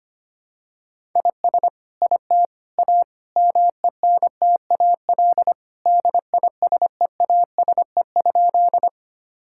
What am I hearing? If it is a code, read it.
Morse code, heard as a keyed tone, about 25 wpm, 700 Hz. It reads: IS IT A MENTAL DISEASE?
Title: Sentences from Top 1000 Words - Review Part 6 - 25wpm